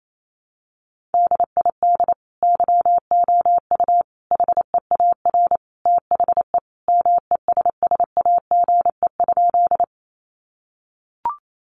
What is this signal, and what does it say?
Morse code, heard as a keyed tone, about 28 wpm, 700 Hz. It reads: DID YOU HEAR THE MESSAGE?
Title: Sentences from Top 1100 Words 28wpm